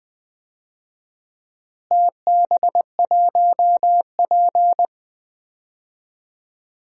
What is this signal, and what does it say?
Morse code, heard as a keyed tone, about 21 wpm, 700 Hz. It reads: TB1P
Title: DXCC-100 Call Signs - 4 Characters - Repeated 3x - 20wpm